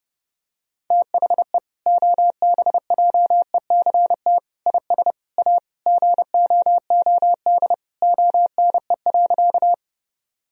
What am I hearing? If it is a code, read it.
Morse code, heard as a keyed tone, about 30 wpm, 700 Hz. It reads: THE OBJECT IS A GOOD ONE.
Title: Sentences from Top 1000 Words - Review Part 3 - 30wpm